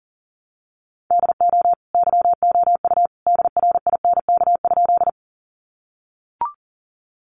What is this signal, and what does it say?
Morse code, heard as a keyed tone, about 40 wpm, 700 Hz. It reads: DO YOU DRINK?
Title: Sentences from Top 1500 Words 40wpm